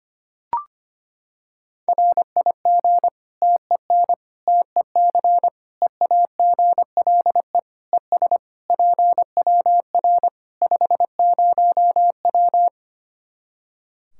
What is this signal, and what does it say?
Morse code, heard as a keyed tone, about 25 wpm, 700 Hz. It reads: RIG TEN TEC EAGLE ES PWR 50W
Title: Practice QSO - Exchange 2 - Info - 25wpm